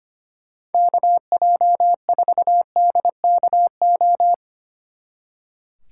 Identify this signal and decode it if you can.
Morse code, heard as a keyed tone, about 30 wpm, 700 Hz. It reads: KJ4DKO
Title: Speed Racing - USA 2x3 Call Signs 25wpm